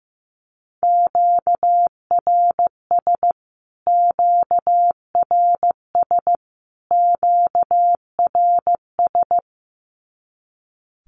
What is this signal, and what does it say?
Morse code, heard as a keyed tone, about 14 wpm, 700 Hz. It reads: QRS QRS QRS
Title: Q-Codes Repeated 3x 15wpm